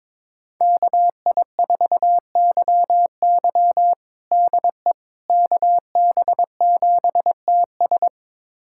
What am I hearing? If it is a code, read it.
Morse code, heard as a keyed tone, about 21 wpm, 700 Hz. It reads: KI4YY DE KB7TS
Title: USA Callsign DE Callsign - 5 Characters - 22wpm